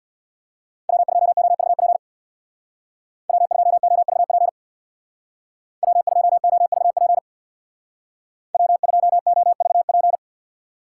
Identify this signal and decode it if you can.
Morse code, heard as a keyed tone, about 62 wpm, 700 Hz. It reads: W2OVP W2OVP W2OVP W2OVP
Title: Speed Racing - USA 1x3 Call Signs 50wpm